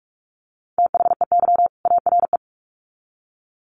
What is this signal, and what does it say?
Morse code, heard as a keyed tone, about 44 wpm, 700 Hz. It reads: THEY ARE
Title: Sets of 2 Words 45wpm